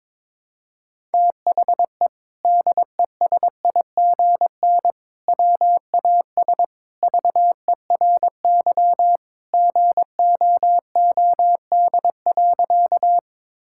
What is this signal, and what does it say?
Morse code, heard as a keyed tone, about 21 wpm, 700 Hz. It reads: THE DESIGN WAS VERY GOOD.
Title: Sentences from Top 1000 Words - Review Part 5 - 22wpm